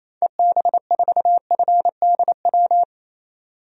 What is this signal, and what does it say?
Morse code, heard as a keyed tone, about 27 wpm, 700 Hz. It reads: EB4FDW
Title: DXCC-100 Call Signs - 6 Characters - 28wpm